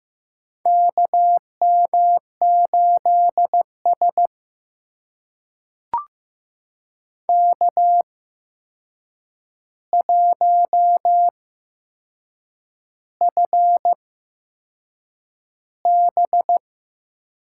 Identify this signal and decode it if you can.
Morse code, heard as a keyed tone, about 15 wpm, spaced as for 6 wpm, 700 Hz. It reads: KM8S K1FB
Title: USA Call Signs - 4 Characters - 8x Character Spacing - 15wpm